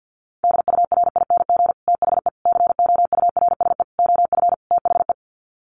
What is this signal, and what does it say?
Morse code, heard as a keyed tone, about 49 wpm, 700 Hz. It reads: DURING THE COURSE OF THE
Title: Sets of 5 Words Encore 50wpm